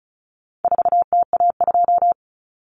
Morse code, heard as a keyed tone, about 34 wpm, 700 Hz. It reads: 4TA2
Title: Common Cut Numbers - 4-Digit 35wpm